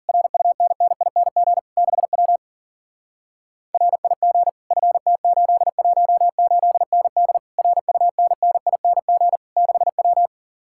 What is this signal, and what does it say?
Morse code, heard as a keyed tone, about 48 wpm, 700 Hz. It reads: RUNNING 6W RIG FT818ND RUNNING 6W
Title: Speed-Racing Common QSO Phrases - 40wpm